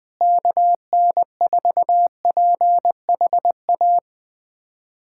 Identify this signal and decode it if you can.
Morse code, heard as a keyed tone, about 20 wpm, 700 Hz. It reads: KN4PHA
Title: USA Callsign DE Callsign 20wpm